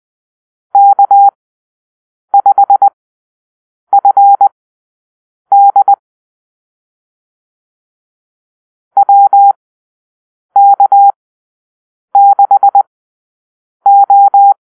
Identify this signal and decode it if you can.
Morse code, heard as a keyed tone, about 20 wpm, spaced as for 7 wpm, 800 Hz. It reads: K5FD WK6O
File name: arrl / Call-Sign Drill-7